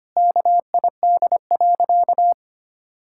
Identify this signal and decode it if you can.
Morse code, heard as a keyed tone, about 25 wpm, 700 Hz. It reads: KID.
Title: Sentences from Top 1600 Words 25wpm